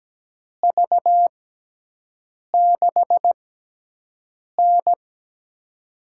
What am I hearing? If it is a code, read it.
Morse code, heard as a keyed tone, about 17 wpm, spaced as for 6 wpm, 700 Hz. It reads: V6N
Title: USA Call Signs - 4 Characters - 6x Character Spacing - 17wpm